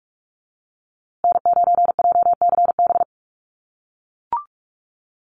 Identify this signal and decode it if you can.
Morse code, heard as a keyed tone, about 44 wpm, 700 Hz. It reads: N9JCB
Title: USA Call Signs Repeated 2x 45wpm